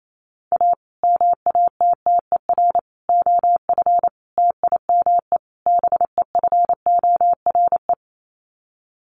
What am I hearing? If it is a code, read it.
Morse code, heard as a keyed tone, about 28 wpm, 700 Hz. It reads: A MATTER OF TIME BEFORE
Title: Sets of 5 Words Encore 28wpm